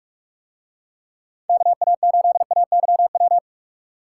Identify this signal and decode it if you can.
Morse code, heard as a keyed tone, about 46 wpm, 700 Hz. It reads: KA8AYW